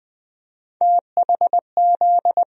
Morse code, heard as a keyed tone, about 20 wpm, 700 Hz. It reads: THZ
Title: USA Call Sign Postfixes - 3 Letters 20wpm